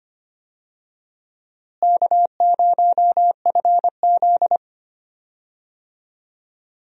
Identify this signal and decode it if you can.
Morse code, heard as a keyed tone, about 25 wpm, 700 Hz. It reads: K0FZ
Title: USA Call Signs - 4 Characters - Repeated 3x - 25wpm